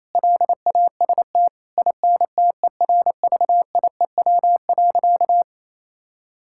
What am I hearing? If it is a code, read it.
Morse code, heard as a keyed tone, about 28 wpm, 700 Hz. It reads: LAST INTERVIEW.
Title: Sentences from Top 1600 Words 28wpm